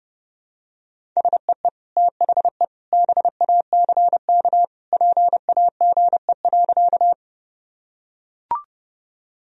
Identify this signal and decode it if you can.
Morse code, heard as a keyed tone, about 32 wpm, 700 Hz. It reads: SEE THE BACK PAGE.
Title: Sentences from Top 1000 Words - Review Part 1 - 30wpm